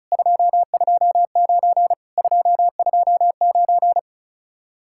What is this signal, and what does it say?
Morse code, heard as a keyed tone, about 43 wpm, 700 Hz. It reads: 229 229
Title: Speed-Racing Common QSO Phrases - 35wpm